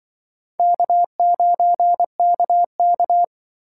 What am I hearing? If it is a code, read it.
Morse code, heard as a keyed tone, about 24 wpm, 700 Hz. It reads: K9KK
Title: Speed Racing - USA 1x2 Call Signs 20wpm